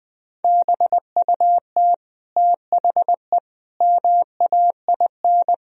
Morse code, heard as a keyed tone, about 20 wpm, 700 Hz. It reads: BUT THE MAIN
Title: Sentences from Top 1000 Words - Review Part 2 - 20wpm